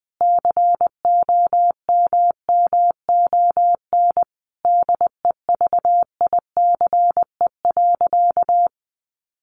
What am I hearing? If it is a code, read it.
Morse code, heard as a keyed tone, about 20 wpm, 700 Hz. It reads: COMMON DEVICE.
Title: Sentences from Top 1500 Words 20wpm